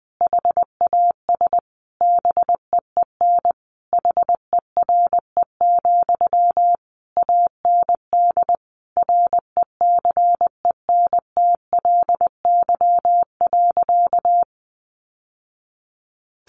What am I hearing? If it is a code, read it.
Morse code, heard as a keyed tone, about 20 wpm, 700 Hz. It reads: HAS BEEN HERE, AND RECENTLY.
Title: Sentences from Top 1200 Words 20wpm